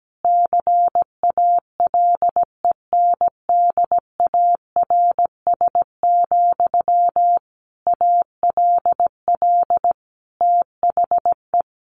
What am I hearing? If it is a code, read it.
Morse code, heard as a keyed tone, about 17 wpm, 700 Hz. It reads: CALENDARS, ALL THE
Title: A Sound Of Thunder 17wpm